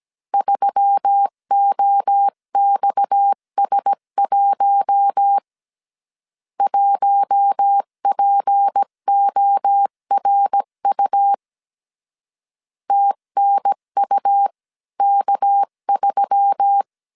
Morse code, heard as a keyed tone, about 17 wpm, 800 Hz. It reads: 3OXS1 1PORU TNUK3
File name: CW / DET / 1 6 1500C S C 16wpm 2733